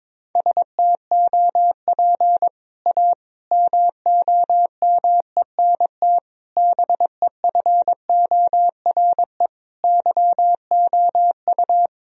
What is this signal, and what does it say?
Morse code, heard as a keyed tone, about 22 wpm, 700 Hz. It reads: STOP A MOMENT BEFORE YOU
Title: Sentences from Top 1000 Words - Review Part 1 - 22wpm